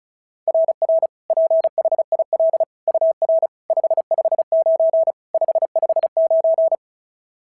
Morse code, heard as a keyed tone, about 36 wpm, 690 Hz. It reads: RR PHIL UR 559 559
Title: SOTA QSO Practice 35wpm